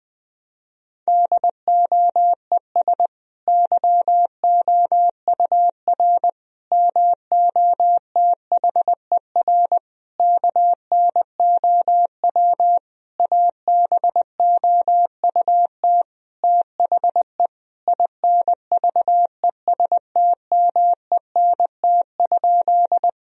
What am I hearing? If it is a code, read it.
Morse code, heard as a keyed tone, about 21 wpm, 700 Hz. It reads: DOES YOUR MOTHER KNOW ABOUT THE INVESTMENT?
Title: Sentences from Top 1000 Words - Review Part 4 - 20wpm